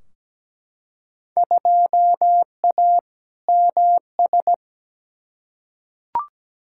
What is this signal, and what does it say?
Morse code, heard as a keyed tone, about 17 wpm, 700 Hz. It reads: 2A MS
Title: Field Day Exchanges - 17wpm